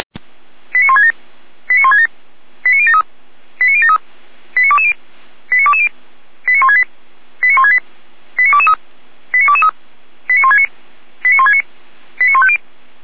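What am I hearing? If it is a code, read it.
Five-tone selective call (ZVEI), heard as quick runs of five tones, tones of about 70 ms. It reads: 89167 89167 89073 89073 89220 89220 89168 89168 89223 89223 89169 89169 89150
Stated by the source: ZVEI five-tone selective call 89060, received via SDR.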